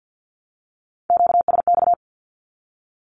Morse code, heard as a keyed tone, about 48 wpm, 700 Hz. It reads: QSX
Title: Q-Codes 50wpm